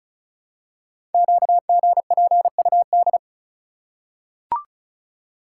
Morse code, heard as a keyed tone, about 36 wpm, 700 Hz. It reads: QGPUD
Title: Code Groups - 5 Letters - 35wpm